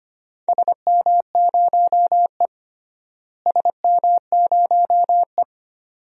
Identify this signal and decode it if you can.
Morse code, heard as a keyed tone, about 25 wpm, 700 Hz. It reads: SM0E SM0E
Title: DXCC-100 Call Signs - 4 Characters - Repeated 2x - 25wpm